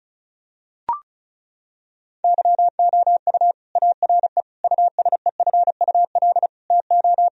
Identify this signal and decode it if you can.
Morse code, heard as a keyed tone, about 34 wpm, 700 Hz. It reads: YOU ARE USEFUL TO
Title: Sentences from Top 1000 Words - Review Part 4 - 35wpm